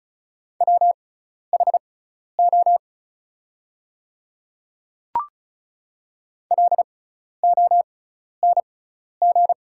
Morse code, heard as a keyed tone, about 36 wpm, spaced as for 12 wpm, 690 Hz. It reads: WHO LONG
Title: 6x Character Spacing - Top 200 Words - 35wpm